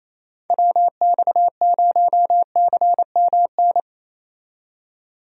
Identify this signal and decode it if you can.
Morse code, heard as a keyed tone, about 28 wpm, 700 Hz. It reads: WX0CMN
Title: USA Call Signs - 6 Characters - 28wpm